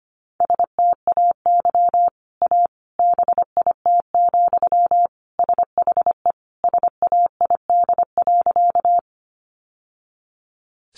Morse code, heard as a keyed tone, about 25 wpm, 700 Hz. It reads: STAY A BIT, SHE SAID.
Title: Sentences from Top 900 Words 25wpm